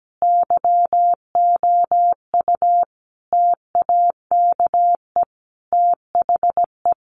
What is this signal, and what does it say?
Morse code, heard as a keyed tone, about 17 wpm, 700 Hz. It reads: YOU TAKE THE
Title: Sentences from Top 1100 Words 17wpm